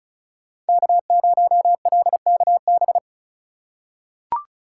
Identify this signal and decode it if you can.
Morse code, heard as a keyed tone, about 34 wpm, 700 Hz. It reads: K0LKB
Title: USA Call Signs - 2x Character Spacing - 35wpm